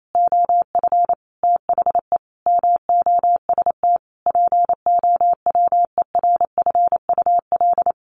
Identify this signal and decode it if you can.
Morse code, heard as a keyed tone, about 28 wpm, 700 Hz. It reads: OF THE MOST POWERFUL
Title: Sets of 5 Words Encore 28wpm